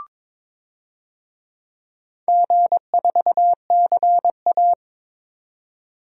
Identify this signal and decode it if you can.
Morse code, heard as a keyed tone, about 22 wpm, 690 Hz. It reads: G4CA
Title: DXCC-100 Call Signs - 4 Characters - 22wpm